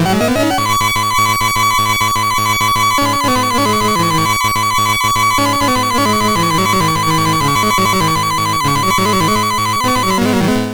There is no code.